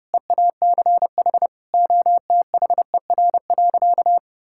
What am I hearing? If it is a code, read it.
Morse code, heard as a keyed tone, about 32 wpm, 700 Hz. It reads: EACH OTHER.
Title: Sentences from Top 1000 Words - Review Part 5 - 30wpm